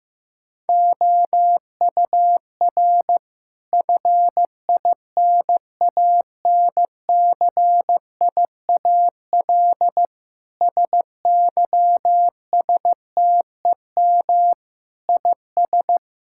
Morse code, heard as a keyed tone, about 16 wpm, 700 Hz. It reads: OUR FINANCIAL SYSTEM IS
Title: Sentences from Top 1000 Words - Review Part 5 - 15wpm